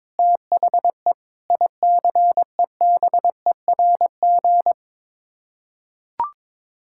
Morse code, heard as a keyed tone, about 21 wpm, 700 Hz. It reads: THE ICEBERG
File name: practice / 22wpm / Sets of 6 Words - 22wpm